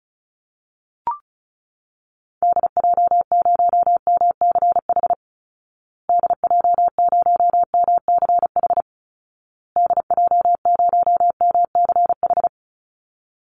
Morse code, heard as a keyed tone, about 34 wpm, 700 Hz. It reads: DJ0MCH DJ0MCH DJ0MCH
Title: Call Signs Repeated 2x - 35wpm